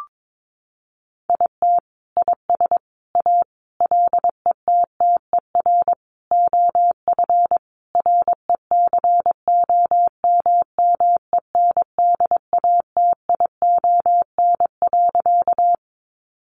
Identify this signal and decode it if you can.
Morse code, heard as a keyed tone, about 22 wpm, 700 Hz. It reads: IT IS A LETTER OF RECOMMENDATION.
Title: Sentences from Top 1400 Words 22wpm